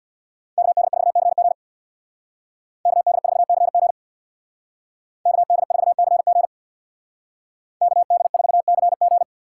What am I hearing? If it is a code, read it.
Morse code, heard as a keyed tone, about 62 wpm, 700 Hz. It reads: KD4CG KD4CG KD4CG KD4CG
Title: Speed Racing - USA 2x2 Call Signs 50wpm